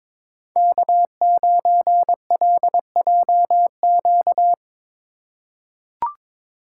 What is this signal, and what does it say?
Morse code, heard as a keyed tone, about 21 wpm, 700 Hz. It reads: K9LJQ
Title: USA Call Signs - 2x Character Spacing - 22wpm